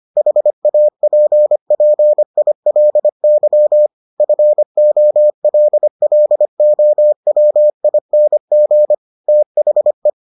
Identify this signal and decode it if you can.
Morse code, heard as a keyed tone, about 25 wpm, 600 Hz. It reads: HAPPILY FOLLOWING THE